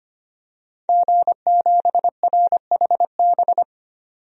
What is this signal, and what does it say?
Morse code, heard as a keyed tone, about 25 wpm, 690 Hz. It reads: G7RHB
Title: DXCC-100 Call Signs - 5 Characters - 25wpm